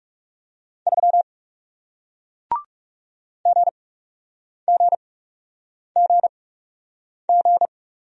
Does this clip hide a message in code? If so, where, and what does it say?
Yes: Morse code from 0.9 to 7.7 s, heard as a keyed tone, about 40 wpm, spaced as for 8 wpm, 700 Hz. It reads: 3 GGGG